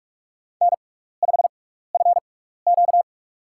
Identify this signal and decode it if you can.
Morse code, heard as a keyed tone, about 43 wpm, spaced as for 15 wpm, 700 Hz. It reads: N5FQ
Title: USA Call Signs - 4 Characters - 6x Character Spacing - 45wpm